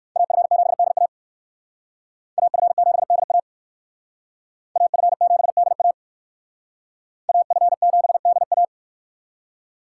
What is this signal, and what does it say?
Morse code, heard as a keyed tone, about 57 wpm, 700 Hz. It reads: AF7DA AF7DA AF7DA AF7DA